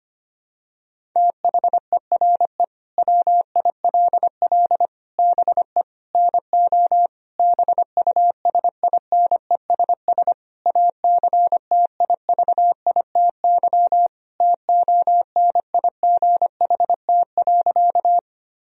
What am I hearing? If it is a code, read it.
Morse code, heard as a keyed tone, about 25 wpm, 700 Hz. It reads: THERE WILL BE NO BUSINESS ACTIVITY TONIGHT.
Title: Sentences from Top 1000 Words - Review Part 4 - 25wpm